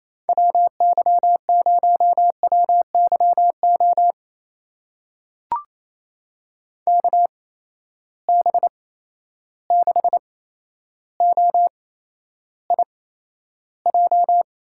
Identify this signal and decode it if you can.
Morse code, heard as a keyed tone, about 28 wpm, spaced as for 12 wpm, 700 Hz. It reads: WY0WYO KB6OIJ
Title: USA Call Signs - 6 Characters - 8x Character Spacing - 28wpm